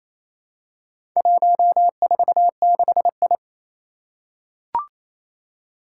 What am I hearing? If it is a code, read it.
Morse code, heard as a keyed tone, about 28 wpm, 700 Hz. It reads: 146I
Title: Four Challenging Letter-Number Combinations 28wpm